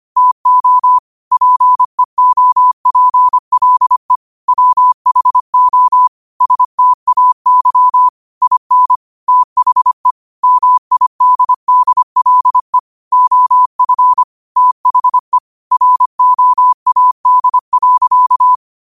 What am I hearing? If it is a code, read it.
Morse code, heard as a keyed tone, about 25 wpm, 1000 Hz. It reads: TO PEOPLE WHO STAY IN THE MIDDLE OF THE ROAD.